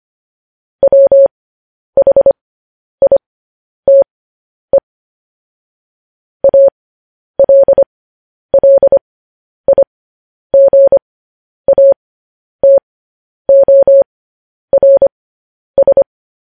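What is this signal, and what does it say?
Morse code, heard as a keyed tone, about 25 wpm, spaced as for 10 wpm, 550 Hz. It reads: WHITE ALLIGATORS
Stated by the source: Morse code news headlines sent at 10 words per minute.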